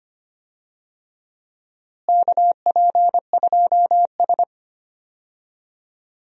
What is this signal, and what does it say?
Morse code, heard as a keyed tone, about 25 wpm, 700 Hz. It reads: KP2S
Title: DXCC-100 Call Signs - 4 Characters - Repeated 3x - 25wpm